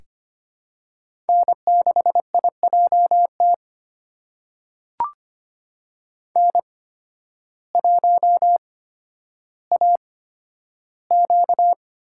Morse code, heard as a keyed tone, about 25 wpm, spaced as for 12 wpm, 700 Hz. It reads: N6IJT N1AQ
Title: USA Call Signs - 5 Characters - 8x Character Spacing - 25wpm